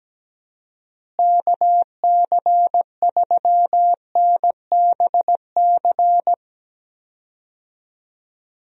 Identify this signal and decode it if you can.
Morse code, heard as a keyed tone, about 17 wpm, 700 Hz. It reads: KC3NBC
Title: USA Call Signs - 6 Characters - Repeated 2x - 17wpm